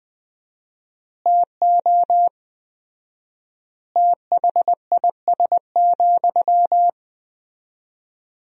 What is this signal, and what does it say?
Morse code, heard as a keyed tone, about 20 wpm, 700 Hz. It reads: TO THIS,
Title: 4x Word-Spacing - Sentences from Top 400 Words - 20wpm